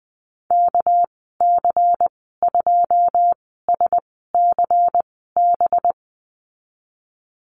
Morse code, heard as a keyed tone, about 20 wpm, spaced as for 15 wpm, 700 Hz. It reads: KC2SCB
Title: 238 Lesson 38g - Letter q - Call Signs 20wpm